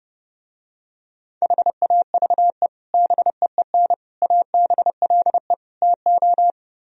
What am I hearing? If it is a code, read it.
Morse code, heard as a keyed tone, about 32 wpm, 700 Hz. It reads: HAVE BEEN ABLE TO